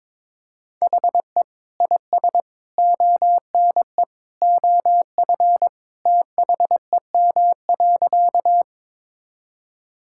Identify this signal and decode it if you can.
Morse code, heard as a keyed tone, about 22 wpm, 700 Hz. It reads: HE IS ONE OF THEM.
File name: practice / 22wpm / Sentences with 3-Word Phrases - 22wpm